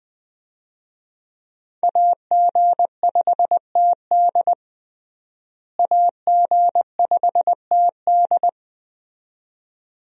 Morse code, heard as a keyed tone, about 20 wpm, 700 Hz. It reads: AG5TD AG5TD
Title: USA Call Signs - 5 Characters - Repeated 2x - 20wpm